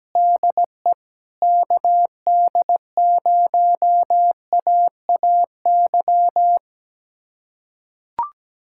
Morse code, heard as a keyed tone, about 17 wpm, 700 Hz. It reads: DE KD0AAY
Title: USA Callsign DE Callsign 17wpm